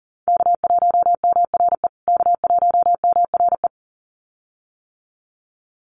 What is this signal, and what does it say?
Morse code, heard as a keyed tone, about 40 wpm, 700 Hz. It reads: K1MRE K1MRE
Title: USA Call Signs Repeated 2x 40wpm